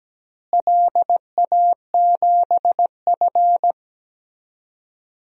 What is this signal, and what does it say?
Morse code, heard as a keyed tone, about 17 wpm, 700 Hz. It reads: LA7F
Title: DXCC-100 Call Signs - 4 Characters - Repeated 3x - 17wpm